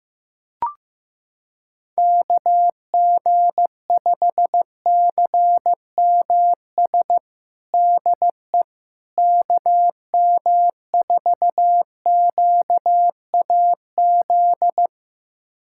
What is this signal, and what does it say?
Morse code, heard as a keyed tone, about 15 wpm, 700 Hz. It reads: KG5CMS DE KM4QAZ
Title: USA Callsign DE Callsign - 6 Characters - 15wpm